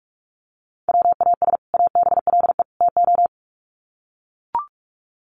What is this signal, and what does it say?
Morse code, heard as a keyed tone, about 44 wpm, 700 Hz. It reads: WAS ABLE TO
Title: Sets of 3 Words 45wpm